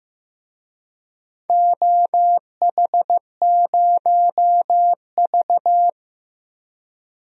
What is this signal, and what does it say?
Morse code, heard as a keyed tone, about 15 wpm, 700 Hz. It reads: OH0V
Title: DXCC-100 Call Signs - 4 Characters - Repeated 2x - 15wpm